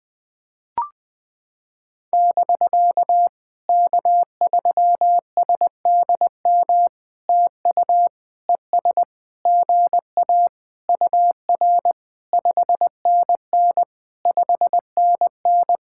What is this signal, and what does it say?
Morse code, heard as a keyed tone, about 20 wpm, 700 Hz. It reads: <BK> K3SDM TU ES GA UR 5NN 5NN
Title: POTA QSO Practice - 20wpm